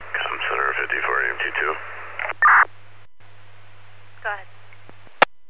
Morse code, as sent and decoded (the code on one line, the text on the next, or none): none